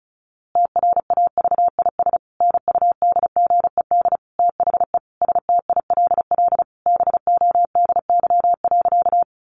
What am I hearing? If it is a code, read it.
Morse code, heard as a keyed tone, about 34 wpm, 700 Hz. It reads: TRAVIS NUDGED THE STILL BODY.
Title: A Sound Of Thunder 35wpm